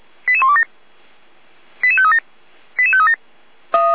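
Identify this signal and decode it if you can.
Five-tone selective call (ZVEI), heard as quick runs of five tones, tones of about 70 ms. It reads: 88127 88537 88537
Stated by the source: ZVEI five-tone selective call received on an SDR.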